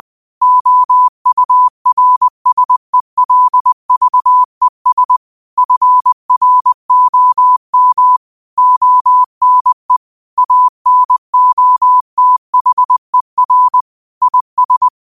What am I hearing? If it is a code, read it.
Morse code, heard as a keyed tone, about 20 wpm, 1000 Hz. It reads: OURSELVES FROM ONE ANOTHER IS